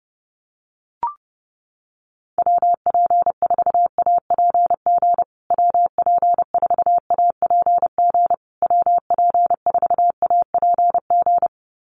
Morse code, heard as a keyed tone, about 30 wpm, 700 Hz. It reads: WP4APG WP4APG WP4APG
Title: USA Call Signs Repeated 3x 30wpm